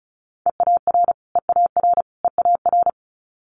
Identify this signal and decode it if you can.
Morse code, heard as a keyed tone, about 36 wpm, 700 Hz. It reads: EAR EAR EAR
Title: Multi-letter Phonemes Repeated 3x 35wpm